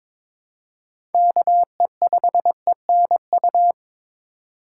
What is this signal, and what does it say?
Morse code, heard as a keyed tone, about 22 wpm, 700 Hz. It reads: KE5ENU
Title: USA Call Signs - 2x Character Spacing - 22wpm